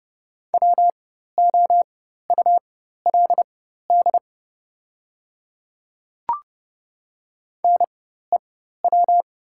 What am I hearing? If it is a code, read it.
Morse code, heard as a keyed tone, about 30 wpm, spaced as for 14 wpm, 700 Hz. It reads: WOULD NEW
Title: 4x Character Spacing - Top 200 Words - 30wpm